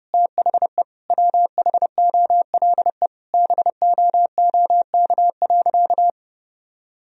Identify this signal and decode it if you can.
Morse code, heard as a keyed tone, about 32 wpm, 700 Hz. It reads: THE WHOLE BOOK.
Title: Sentences from Top 1000 Words - Review Part 1 - 30wpm